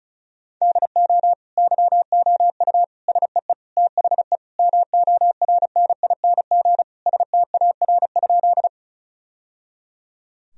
Morse code, heard as a keyed tone, about 34 wpm, 700 Hz. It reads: DO YOU SEE THE MORNING STAR?
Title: Sentences from Top 1000 Words - Review Part 2 - 35wpm